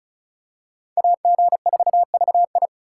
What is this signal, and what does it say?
Morse code, heard as a keyed tone, about 34 wpm, 700 Hz. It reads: AG4VI
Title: USA Call Signs - 5 Characters - 6x Character Spacing - 35wpm